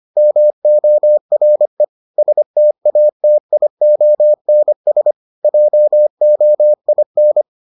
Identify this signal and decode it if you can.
Morse code, heard as a keyed tone, about 25 wpm, 600 Hz. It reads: MORE STATIONS JOIN